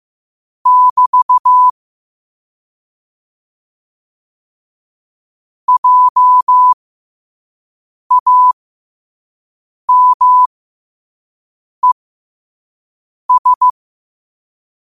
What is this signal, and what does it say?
Morse code, heard as a keyed tone, about 15 wpm, spaced as for 5 wpm, 1000 Hz. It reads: <BT> JAMES
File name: qotdfiles / 5wpm